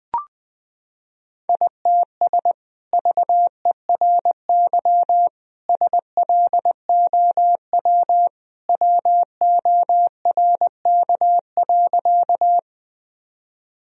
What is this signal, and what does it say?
Morse code, heard as a keyed tone, about 20 wpm, 700 Hz. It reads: ITS VERY SLOW WORK.
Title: Sentences from Top 1000 Words - Review Part 2 - 20wpm